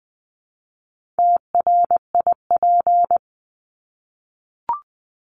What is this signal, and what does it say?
Morse code, heard as a keyed tone, about 20 wpm, 700 Hz. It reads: TRIP
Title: Top 1251-1500 Words 20wpm